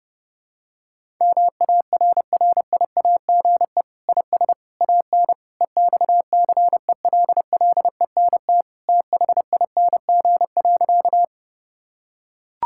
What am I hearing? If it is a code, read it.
Morse code, heard as a keyed tone, about 32 wpm, 700 Hz. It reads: MARRIAGE IS AN EXCELLENT THING.
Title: Sentences from Top 1000 Words - Review Part 5 - 30wpm